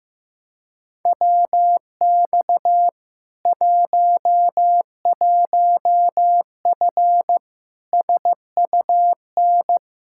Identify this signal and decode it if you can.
Morse code, heard as a keyed tone, about 15 wpm, 700 Hz. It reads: WX 11F SUN
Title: Practice QSO - Exchange 2 - Info - 15wpm